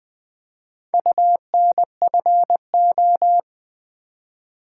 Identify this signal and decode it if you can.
Morse code, heard as a keyed tone, about 20 wpm, 700 Hz. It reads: UNFO
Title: Build A Word - 4 Letters At A Time - 11-Letter Words - 20wpm